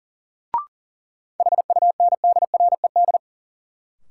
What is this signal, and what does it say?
Morse code, heard as a keyed tone, about 37 wpm, 700 Hz. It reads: HUNDRED
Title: Sentences from Top 400 Words - New Word Drill - 40wpm